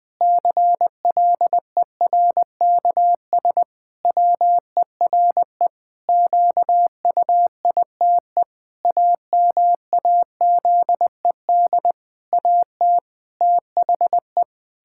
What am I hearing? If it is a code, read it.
Morse code, heard as a keyed tone, about 21 wpm, 700 Hz. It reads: CLERKS WERE QUITE AMAZED AT THE
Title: Pangram Challenge v2 - 20wpm